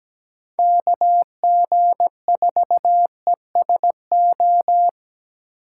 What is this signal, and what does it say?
Morse code, heard as a keyed tone, about 20 wpm, 700 Hz. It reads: KG4ESO
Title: Speed Racing - USA 2x3 Call Signs 17wpm